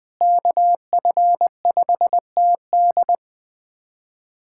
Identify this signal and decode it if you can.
Morse code, heard as a keyed tone, about 20 wpm, 700 Hz. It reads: KF5TD
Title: USA Callsign DE Callsign - 5 Characters - 20wpm